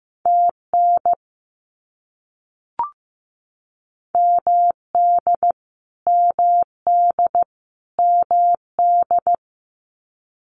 Morse code, heard as a keyed tone, about 15 wpm, 700 Hz. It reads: TN MD MD MD
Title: US States Repeated 3x 15wpm